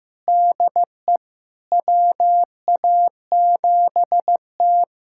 Morse code, heard as a keyed tone, about 15 wpm, 700 Hz. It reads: DE WA7T
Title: USA Callsign DE Callsign - 4 Characters - 15wpm